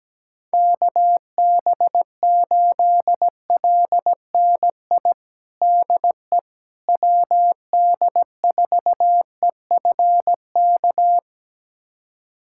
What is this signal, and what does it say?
Morse code, heard as a keyed tone, about 17 wpm, 700 Hz. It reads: KB8LNI DE WD4EFK
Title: USA Callsign DE Callsign 17wpm